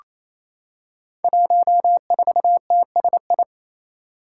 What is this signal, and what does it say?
Morse code, heard as a keyed tone, about 28 wpm, 700 Hz. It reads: 14TSI
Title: Five Challenging Letter-Number Combinations 28wpm